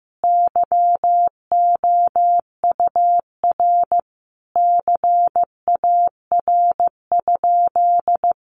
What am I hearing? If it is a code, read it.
Morse code, heard as a keyed tone, about 15 wpm, 700 Hz. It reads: YOUR CAR?
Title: Sentences from Top 1500 Words 15wpm